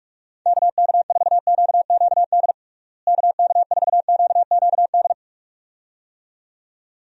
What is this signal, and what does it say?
Morse code, heard as a keyed tone, about 46 wpm, 700 Hz. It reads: KK4QQD KK4QQD
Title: USA Call Signs - 6 Characters - Repeated 2x - 45wpm